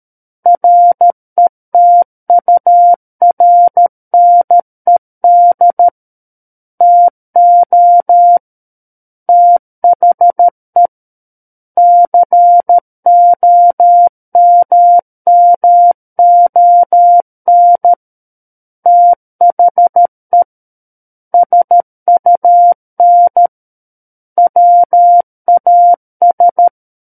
Morse code, heard as a keyed tone, about 13 wpm, 700 Hz. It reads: RETURNED TO THE COMMON THE SUN WAS
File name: Book One / 04-Chapter 4 - 13 WPM